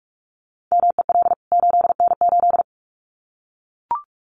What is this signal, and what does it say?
Morse code, heard as a keyed tone, about 44 wpm, 700 Hz. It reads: MEL 8N8
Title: CWT Exchange 45wpm